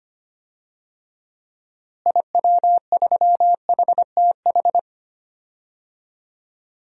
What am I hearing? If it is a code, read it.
Morse code, heard as a keyed tone, about 25 wpm, 700 Hz. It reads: IW3HTH
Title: DXCC-100 Call Signs - 6 Characters - Repeated 3x - 25wpm